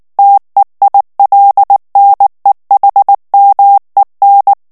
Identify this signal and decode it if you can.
Morse code, heard as a keyed tone, about 19 wpm, 800 Hz. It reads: TEILNEHMEN
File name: morse mp3 / ALLEMAND texte clair 2 vitesse 18 mots minute